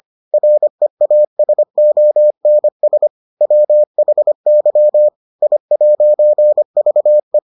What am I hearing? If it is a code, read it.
Morse code, heard as a keyed tone, about 25 wpm, 600 Hz. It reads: REASONS WHY I'VE